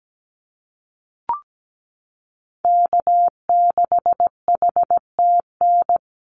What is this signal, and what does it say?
Morse code, heard as a keyed tone, about 17 wpm, 700 Hz. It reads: K6HTN
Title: Call Signs - 17wpm